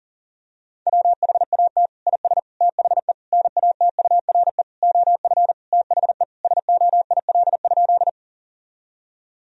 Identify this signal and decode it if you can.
Morse code, heard as a keyed tone, about 37 wpm, 700 Hz. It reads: WHAT IS THE NATURE OF THE SOIL?
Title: Sentences from Top 1000 Words - Review Part 4 - 40wpm